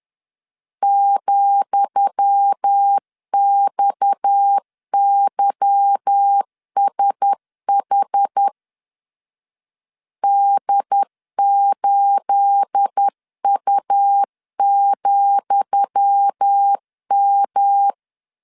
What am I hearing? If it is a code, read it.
Morse code, heard as a keyed tone, about 10 wpm, 800 Hz. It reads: ,XYSH D8U,M
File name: LEZIONI / 10 SIGNS / 10D COMA 10wpm 3013